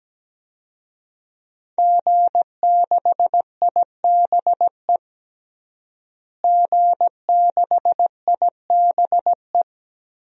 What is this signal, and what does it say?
Morse code, heard as a keyed tone, about 17 wpm, 700 Hz. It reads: G6IBE G6IBE
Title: DXCC-100 Call Signs - 5 Characters - Repeated 2x - 17wpm